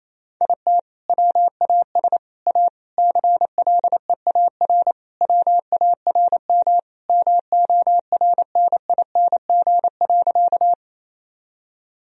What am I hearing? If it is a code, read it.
Morse code, heard as a keyed tone, about 28 wpm, 700 Hz. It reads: IT WAS A CLEAR WARM MORNING.
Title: Sentences from Top 1000 Words - Review Part 2 - 28wpm